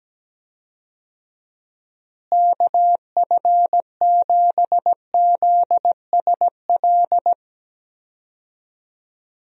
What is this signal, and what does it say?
Morse code, heard as a keyed tone, about 17 wpm, 700 Hz. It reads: KF7ZSL
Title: USA Call Signs - 6 Characters - Repeated 2x - 17wpm